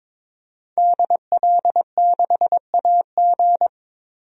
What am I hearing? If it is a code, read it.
Morse code, heard as a keyed tone, about 22 wpm, 700 Hz. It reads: DL6AG
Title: DXCC-100 Call Signs - 5 Characters - 22wpm